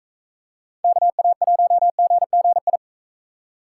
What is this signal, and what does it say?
Morse code, heard as a keyed tone, about 41 wpm, 700 Hz. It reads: KA1GGI